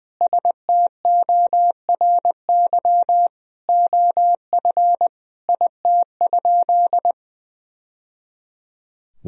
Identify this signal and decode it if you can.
Morse code, heard as a keyed tone, about 20 wpm, 700 Hz. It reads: STORY OF IT?
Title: Sentences from Top 1000 Words - Review Part 1 - 20wpm